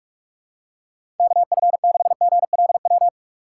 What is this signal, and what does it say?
Morse code, heard as a keyed tone, about 47 wpm, 700 Hz. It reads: KF6GLW
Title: USA Call Signs - 6 Characters - 2x Character Spacing - 45wpm